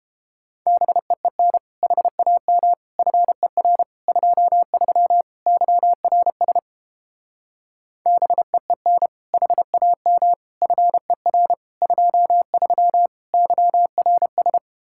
Morse code, heard as a keyed tone, about 31 wpm, 700 Hz. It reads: BEEN HAM FER 23 YRS BEEN HAM FER 23 YRS